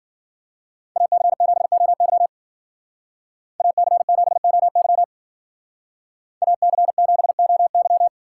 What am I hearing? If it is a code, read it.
Morse code, heard as a keyed tone, about 48 wpm, 700 Hz. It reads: AC7OY AC7OY AC7OY
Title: Speed Racing - USA 2x2 Call Signs 40wpm